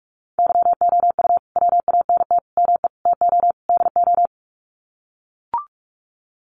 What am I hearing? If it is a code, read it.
Morse code, heard as a keyed tone, about 44 wpm, 700 Hz. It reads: YOU WANT ME TO DO
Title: Sets of 5 Words Encore2 45wpm